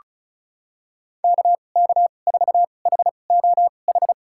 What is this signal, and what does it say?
Morse code, heard as a keyed tone, about 36 wpm, spaced as for 25 wpm, 700 Hz. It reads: KK4HOH
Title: USA Call Signs - 6 Characters - 2x Character Spacing - 35wpm